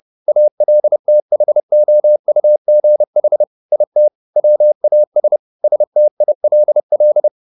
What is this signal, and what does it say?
Morse code, heard as a keyed tone, about 30 wpm, 600 Hz. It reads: ALTHOUGH IT WAS STILL
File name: the old man and the sea / the old man and the sea 1 30wpm